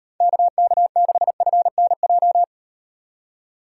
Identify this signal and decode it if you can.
Morse code, heard as a keyed tone, about 38 wpm, 700 Hz. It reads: KK6FNJ